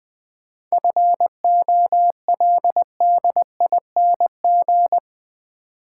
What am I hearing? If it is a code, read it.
Morse code, heard as a keyed tone, about 20 wpm, 700 Hz. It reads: FOLDING
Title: 7 Letter Words - First 3 Letters Given - 20wpm